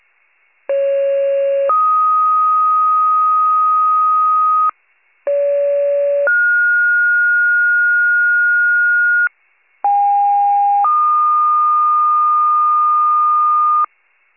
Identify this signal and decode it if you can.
Two-tone page, heard as a one-second tone, then a three-second tone, about 550 then 1250 Hz, about 570 then 1430 Hz, about 810 then 1220 Hz. It is heard from 0.7 to 13.9 s.
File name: pager / 73 RESCUE / wcalert